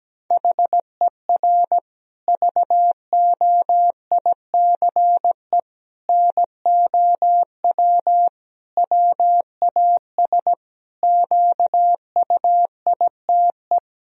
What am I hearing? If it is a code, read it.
Morse code, heard as a keyed tone, about 17 wpm, 700 Hz. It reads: HER VOICE NOW WAS QUITE
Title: Sentences from Top 1000 Words - Review Part 3 - 17wpm